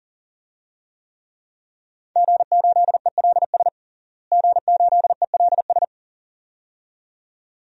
Morse code, heard as a keyed tone, about 37 wpm, 700 Hz. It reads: G8ELS G8ELS
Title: DXCC-100 Call Signs - 5 Characters - Repeated 2x - 40wpm